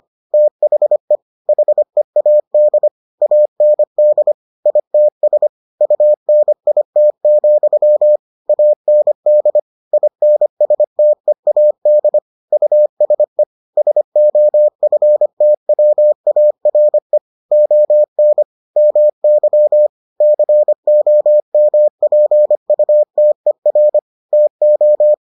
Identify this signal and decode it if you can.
Morse code, heard as a keyed tone, about 25 wpm, 600 Hz. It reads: THE HEAD AND ITS UNIT, AND INSTEAD USE SOFTWARE ON MY COMPUTER TO